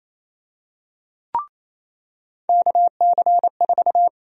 Morse code, heard as a keyed tone, about 27 wpm, 700 Hz. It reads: KC4
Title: USA Call Sign Prefixes - 28wpm